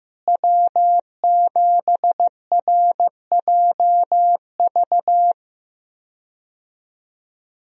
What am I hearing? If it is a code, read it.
Morse code, heard as a keyed tone, about 15 wpm, 700 Hz. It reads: W7RJV
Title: USA Call Signs - 5 Characters - Repeated 3x - 15wpm